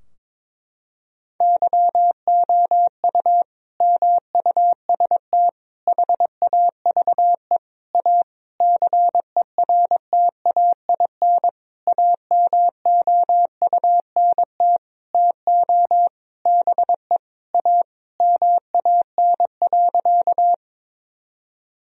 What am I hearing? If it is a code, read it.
Morse code, heard as a keyed tone, about 22 wpm, 700 Hz. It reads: YOU MUST HAVE A CERTAIN AMOUNT TO BE A MAN.
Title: Sentences from Top 1000 Words - Review Part 4 - 22wpm